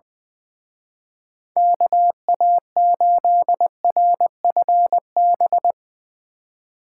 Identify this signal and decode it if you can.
Morse code, heard as a keyed tone, about 22 wpm, 700 Hz. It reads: KA8RFB
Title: Speed Racing - USA 2x3 Call Signs 20wpm